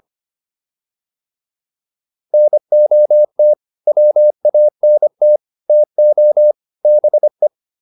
Morse code, heard as a keyed tone, about 25 wpm, 600 Hz. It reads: NOT WANT TO BE
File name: WD304 25